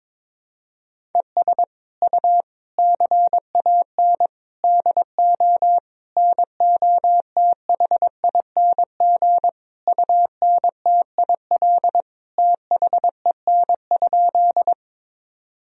Morse code, heard as a keyed tone, about 22 wpm, 700 Hz. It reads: ES U CAN DO NOTHING UNTIL THEN?
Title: Sentences from Top 500 Words and Abbreviations 22wpm